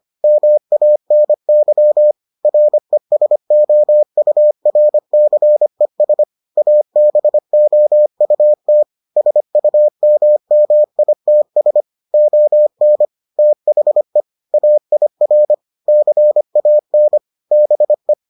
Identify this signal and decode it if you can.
Morse code, heard as a keyed tone, about 25 wpm, 600 Hz. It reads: MANY RESOURCES ABOUT SUMMITS ON THE AIR CAN BE